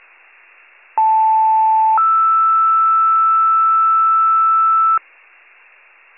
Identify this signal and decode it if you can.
Two-tone page, heard as a one-second tone, then a three-second tone, about 880 then 1320 Hz. It is heard from 1.0 to 5.0 s.